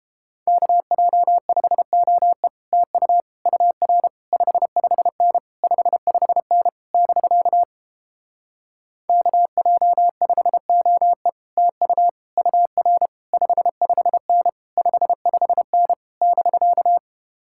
Morse code, heard as a keyed tone, about 31 wpm, 700 Hz. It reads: KJ5OE TU UR 55N 55N <BK> KJ5OE TU UR 55N 55N <BK>